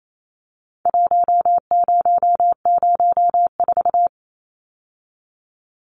Morse code, heard as a keyed tone, about 28 wpm, 700 Hz. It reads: 1004
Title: Numbers - Contest Style - 4-Digit 28wpm